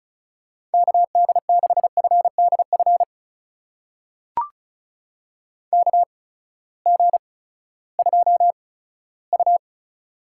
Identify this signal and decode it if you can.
Morse code, heard as a keyed tone, about 34 wpm, spaced as for 17 wpm, 700 Hz. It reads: KD6FDF KG2U
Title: USA Call Signs - 8x Character Spacing - 35wpm